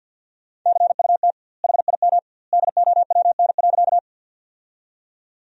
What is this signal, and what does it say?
Morse code, heard as a keyed tone, about 51 wpm, 700 Hz. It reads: CUT HIM DOWN.
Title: 2x Word-Spacing - Sentences from Top 300 Words - 50wpm